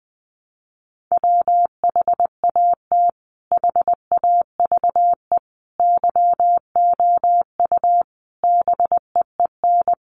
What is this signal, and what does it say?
Morse code, heard as a keyed tone, about 20 wpm, 700 Hz. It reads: WHAT HAVE YOU BEEN
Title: Sentences from Top 200 Words 20wpm